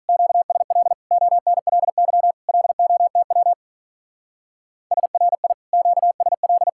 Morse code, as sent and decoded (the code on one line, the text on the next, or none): --.- ... .-.. / --- -. .-.. -.-- / .-.. --- - .-- / ... .-. .. / --.- ... .-..
QSL ONLY LOTW SRI QSL